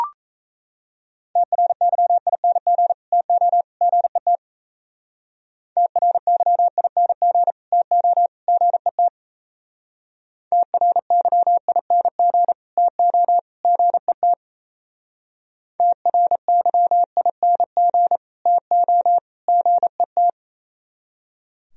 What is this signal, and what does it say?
Morse code, heard as a keyed tone, about 35 wpm, 700 Hz. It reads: TRYING TO GET TRYING TO GET TRYING TO GET TRYING TO GET
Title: Speed-Racing - Sets of 3 Words 28wpm